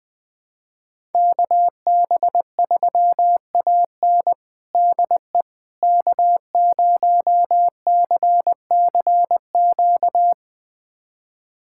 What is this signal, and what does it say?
Morse code, heard as a keyed tone, about 20 wpm, 700 Hz. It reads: KB3AN DE K0CCQ
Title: USA Callsign DE Callsign - 5 Characters - 20wpm